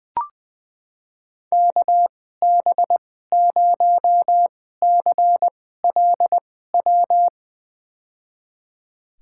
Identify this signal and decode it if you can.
Morse code, heard as a keyed tone, about 21 wpm, spaced as for 15 wpm, 700 Hz. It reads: KB0CLW
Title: USA Call Signs - 2x Character Spacing - 20wpm